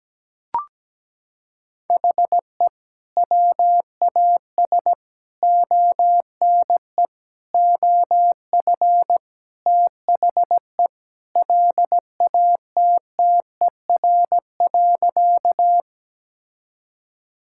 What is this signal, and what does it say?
Morse code, heard as a keyed tone, about 17 wpm, 700 Hz. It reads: HE WAS ONE OF THE LATTER.
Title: Sentences from Top 1600 Words 17wpm